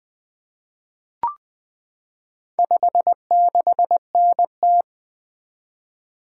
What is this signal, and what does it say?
Morse code, heard as a keyed tone, about 20 wpm, 700 Hz. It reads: 56NT
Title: Four Letter-Number Combinations - 20wpm